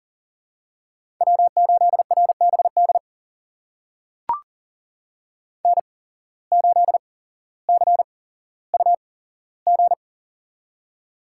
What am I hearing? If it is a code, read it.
Morse code, heard as a keyed tone, about 40 wpm, spaced as for 17 wpm, 700 Hz. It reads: W8RBD N8CUG
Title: USA Call Signs - 5 Characters - 8x Character Spacing - 40wpm